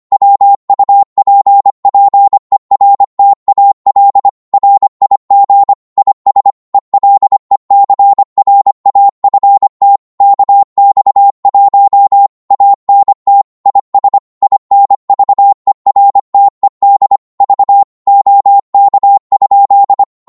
Morse code, heard as a keyed tone, about 25 wpm, 800 Hz. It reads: WUPPERTAL RIG IS ELECRAFT KX1 ANT IS INVERTED V OK?
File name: Morse / qso-25wpm